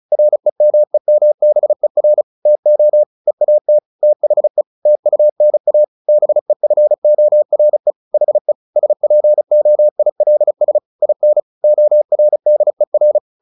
Morse code, heard as a keyed tone, about 34 wpm, 600 Hz. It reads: REMEMBER TO EAT THE TUNA BEFORE HE SPOILS IN ORDER
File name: the old man and the sea / the old man and the sea 1 35wpm